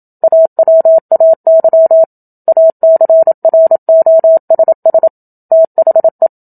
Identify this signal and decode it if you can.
Morse code, heard as a keyed tone, about 27 wpm, 650 Hz. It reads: AWAY ACROSS THE
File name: Book Two / 01-Chapter 1 - 27 WPM